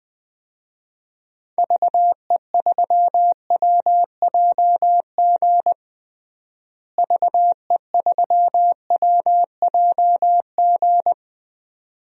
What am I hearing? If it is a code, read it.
Morse code, heard as a keyed tone, about 20 wpm, 700 Hz. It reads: VE3WJG VE3WJG
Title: DXCC-100 Call Signs - 6 Characters - Repeated 2x - 20wpm